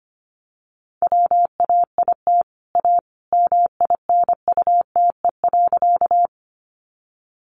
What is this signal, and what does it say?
Morse code, heard as a keyed tone, about 25 wpm, 700 Hz. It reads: WAIT A MINUTE.
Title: Sentences from Top 500 Words 25wpm